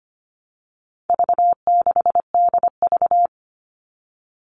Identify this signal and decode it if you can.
Morse code, heard as a keyed tone, about 25 wpm, 700 Hz. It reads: V6DV
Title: All Cut Numbers - 4-Digit 25wpm